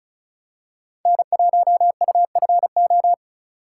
Morse code, heard as a keyed tone, about 36 wpm, 700 Hz. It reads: N1UFO
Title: USA Call Signs - 5 Characters - 2x Character Spacing - 35wpm